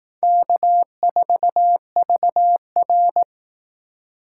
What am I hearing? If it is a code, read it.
Morse code, heard as a keyed tone, about 18 wpm, 700 Hz. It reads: K4VR
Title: Speed Racing - USA Call Signs 15wpm